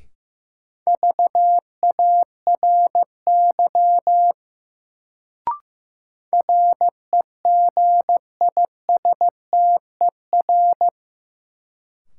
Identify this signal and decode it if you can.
Morse code, heard as a keyed tone, about 15 wpm, 700 Hz. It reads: VARY REGISTER
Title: Sentences from Top 1200 Words - New Word Drill Alt - 15wpm